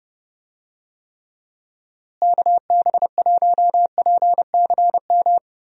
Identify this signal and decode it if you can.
Morse code, heard as a keyed tone, about 32 wpm, 700 Hz. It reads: KB1PCM
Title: USA Call Signs - 6 Characters - Repeated 3x - 30wpm